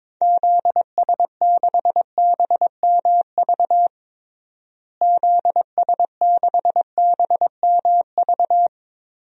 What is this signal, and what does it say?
Morse code, heard as a keyed tone, about 21 wpm, 700 Hz. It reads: ZS6BMV ZS6BMV
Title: DXCC-100 Call Signs - 6 Characters - Repeated 3x - 22wpm